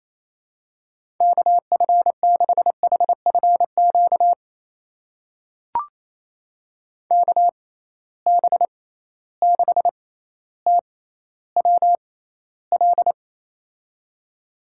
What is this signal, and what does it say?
Morse code, heard as a keyed tone, about 28 wpm, spaced as for 14 wpm, 700 Hz. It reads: KF6HFQ KB6TWL
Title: USA Call Signs - 6 Characters - 6x Character Spacing - 28wpm